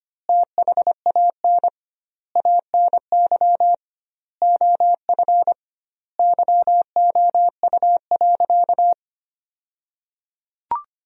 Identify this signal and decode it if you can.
Morse code, heard as a keyed tone, about 25 wpm, 700 Hz. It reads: THAN ANY OF YOU.
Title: 2x Word-Spacing - Sentences from Top 400 Words - 25wpm